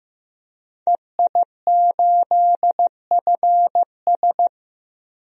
Morse code, heard as a keyed tone, about 15 wpm, 700 Hz. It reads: EI8FS
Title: DXCC-100 Call Signs - 5 Characters - Repeated 3x - 15wpm